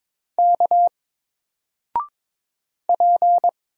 Morse code, heard as a keyed tone, about 22 wpm, 700 Hz. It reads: K P
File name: Single-Letter-Number-22wpm